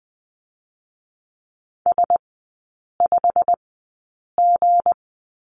Morse code, heard as a keyed tone, about 20 wpm, 700 Hz. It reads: S 5 G
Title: 218 Lesson 35e - Number 0 - Three Characters 20wpm